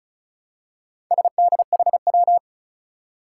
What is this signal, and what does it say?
Morse code, heard as a keyed tone, about 36 wpm, 700 Hz. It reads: SDHW
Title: Four Challenging Letter-Number Combinations 35wpm